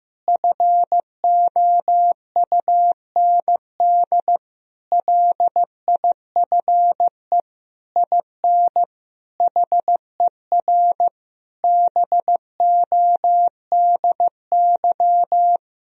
Morse code, heard as a keyed tone, about 15 wpm, 700 Hz. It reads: FOUND LIFE IN HER BODY
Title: Sentences from Top 1000 Words - Review Part 2 - 15wpm